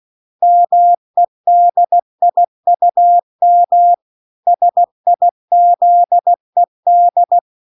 Morse code, heard as a keyed tone, about 16 wpm, 700 Hz. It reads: MEDIUM SIZED
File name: 16WPM